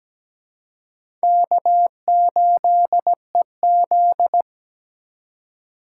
Morse code, heard as a keyed tone, about 20 wpm, 700 Hz. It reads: K8EZ
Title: Speed Racing - USA Call Signs 17wpm